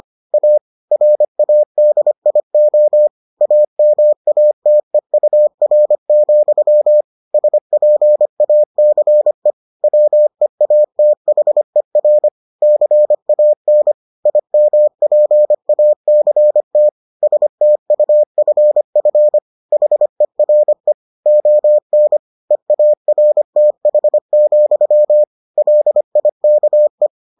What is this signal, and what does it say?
Morse code, heard as a keyed tone, about 25 wpm, 600 Hz. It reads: A RADIO AMATEUR, SPACE WEATHER CAN IMPACT STUFF HERE ON EARTH, LIKE